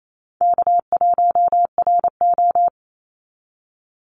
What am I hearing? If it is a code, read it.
Morse code, heard as a keyed tone, about 28 wpm, 700 Hz. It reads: K1RO
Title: Call Signs Repeated 3x - 28wpm